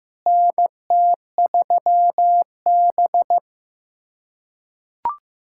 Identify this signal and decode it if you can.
Morse code, heard as a keyed tone, about 15 wpm, 700 Hz. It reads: NT3B
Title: USA Callsign DE Callsign - 4 Characters - 15wpm